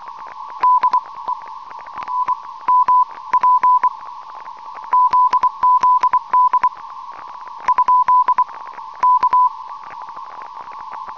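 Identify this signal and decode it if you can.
Morse code, heard as a keyed tone, about 24 wpm, spaced as for 19 wpm, 1000 Hz. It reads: DE NMP ZZD ? K
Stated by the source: Unidentified CW signal on 5167.5 kHz heard by John OZ3AM.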